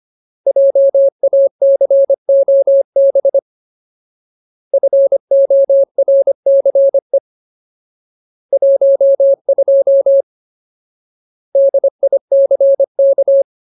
Morse code, heard as a keyed tone, about 25 wpm, 550 Hz. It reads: JACOB FORCE 12 DICK
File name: Ultimate Word List / 25WPM Ultimate